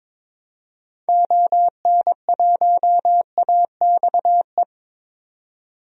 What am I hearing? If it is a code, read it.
Morse code, heard as a keyed tone, about 22 wpm, 700 Hz. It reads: ON1AXE
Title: DXCC-100 Call Signs - 6 Characters - 22wpm